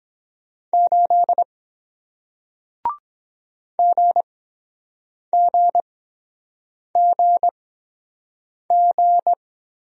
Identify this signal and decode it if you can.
Morse code, heard as a keyed tone, about 23 wpm, spaced as for 7 wpm, 700 Hz. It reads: 8 GGGG